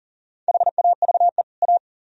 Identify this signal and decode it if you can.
Morse code, heard as a keyed tone, about 51 wpm, 700 Hz. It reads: HAVE A